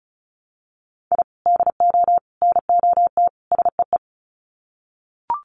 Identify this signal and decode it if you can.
Morse code, heard as a keyed tone, about 34 wpm, 700 Hz. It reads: I DO NOT SEE
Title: Sets of 4 Words 35wpm